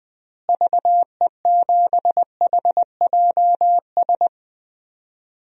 Morse code, heard as a keyed tone, about 20 wpm, 700 Hz. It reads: VE7HJS
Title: DXCC-100 Call Signs - 6 Characters - Repeated 2x - 20wpm